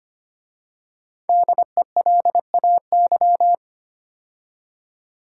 Morse code, heard as a keyed tone, about 25 wpm, 700 Hz. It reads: DELAY
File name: 5 Letter Words - First 3 Letters Given - 25wpm